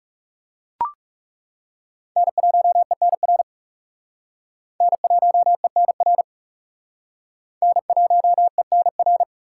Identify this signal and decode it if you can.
Morse code, heard as a keyed tone, about 38 wpm, 700 Hz. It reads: N1ENR N1ENR N1ENR